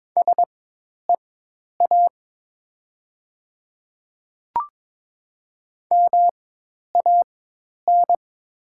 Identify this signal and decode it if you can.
Morse code, heard as a keyed tone, about 22 wpm, spaced as for 10 wpm, 700 Hz. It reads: SEA MAN
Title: 4x Character Spacing - Top 500 Words - 22wpm